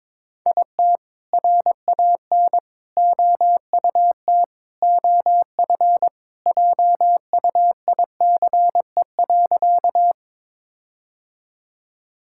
Morse code, heard as a keyed tone, about 22 wpm, 690 Hz. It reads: IT RAN OUT OF JUICE.
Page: morse-code-ninja-practice-22wpm